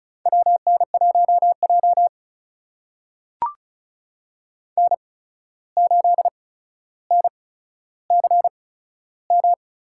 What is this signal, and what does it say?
Morse code, heard as a keyed tone, about 36 wpm, spaced as for 14 wpm, 700 Hz. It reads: WN1J N8NCM
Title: USA Call Signs - 8x Character Spacing - 35wpm